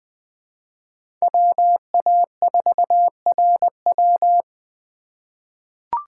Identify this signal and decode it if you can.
Morse code, heard as a keyed tone, about 21 wpm, 700 Hz. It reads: WA4RW
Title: USA Call Signs - 5 Characters - 2x Character Spacing - 20wpm